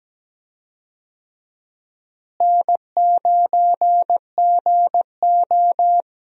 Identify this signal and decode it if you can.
Morse code, heard as a keyed tone, about 17 wpm, 700 Hz. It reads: N9GO
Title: USA Call Signs - 4 Characters - Repeated 3x - 17wpm